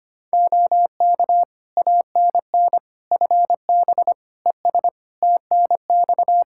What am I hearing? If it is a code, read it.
Morse code, heard as a keyed tone, about 25 wpm, 700 Hz. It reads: OK ANN FB ES TNX
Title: K1USN SST QSO Practice 25wpm